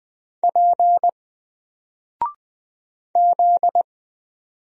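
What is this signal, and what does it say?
Morse code, heard as a keyed tone, about 20 wpm, 700 Hz. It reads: P Z
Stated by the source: Single letter and number Morse drill at 20 wpm, from N4LCD.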